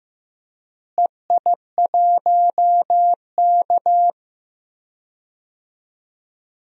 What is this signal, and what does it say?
Morse code, heard as a keyed tone, about 15 wpm, 700 Hz. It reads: EI1K
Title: DXCC-100 Call Signs - 4 Characters - Repeated 2x - 15wpm